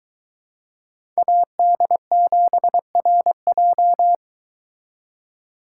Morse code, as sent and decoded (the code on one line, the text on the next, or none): .- -.. --... .-. .---
AD7RJ